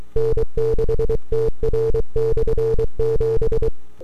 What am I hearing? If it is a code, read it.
Morse code, heard as a keyed tone, about 23 wpm, 450 Hz. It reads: N6TR/7
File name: n6tr / 12Oct2009-160